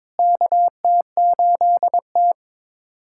Morse code, heard as a keyed tone, about 22 wpm, 700 Hz. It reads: KT8T
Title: USA Callsign DE Callsign - 4 Characters - 22wpm